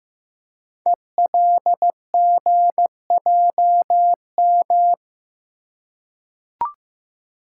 Morse code, heard as a keyed tone, about 15 wpm, 700 Hz. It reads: ELGJM
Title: Code Groups - 5 Letters - 15wpm